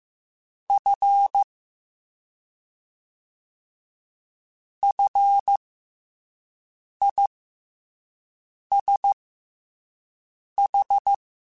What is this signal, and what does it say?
Morse code, heard as a keyed tone, about 15 wpm, spaced as for 5 wpm, 800 Hz. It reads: F FISH
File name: F and L / F and L mp3 / F and L Prefixes 3x2 Twice @5